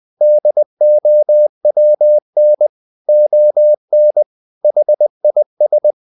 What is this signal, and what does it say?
Morse code, heard as a keyed tone, about 20 wpm, 600 Hz. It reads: DOWN ON HIS
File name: the old man and the sea / the old man and the sea part 1 20wpm